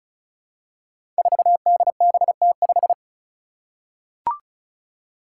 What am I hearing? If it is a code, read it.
Morse code, heard as a keyed tone, about 36 wpm, 700 Hz. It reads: 4DBT5
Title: Five Challenging Letter-Number Combinations 35wpm